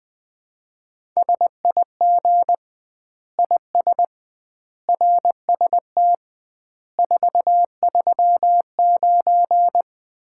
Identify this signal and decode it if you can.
Morse code, heard as a keyed tone, about 20 wpm, 700 Hz. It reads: SIG IS RST 439
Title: Common QSO Phrases - 2x Word-Spacing - 20wpm